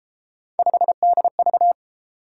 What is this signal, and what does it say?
Morse code, heard as a keyed tone, about 31 wpm, 700 Hz. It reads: 5DV